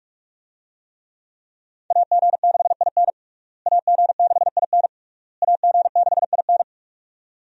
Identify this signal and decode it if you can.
Morse code, heard as a keyed tone, about 46 wpm, 700 Hz. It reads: AG6IN AG6IN AG6IN
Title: USA Call Signs - 5 Characters - Repeated 3x - 45wpm